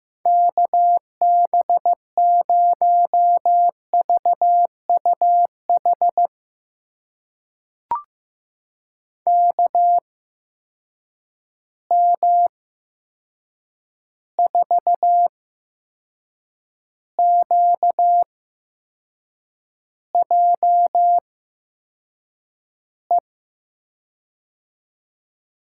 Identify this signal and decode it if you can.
Morse code, heard as a keyed tone, about 15 wpm, spaced as for 6 wpm, 700 Hz. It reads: KB0VUH KM4QJE
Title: USA Call Signs - 6 Characters - 8x Character Spacing - 15wpm